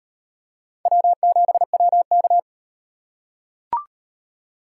Morse code, heard as a keyed tone, about 38 wpm, 700 Hz. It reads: W7WK